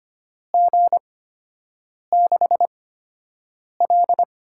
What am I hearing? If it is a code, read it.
Morse code, heard as a keyed tone, about 25 wpm, spaced as for 7 wpm, 700 Hz. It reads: G6L
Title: USA Call Signs - 4 Characters - 8x Character Spacing - 25wpm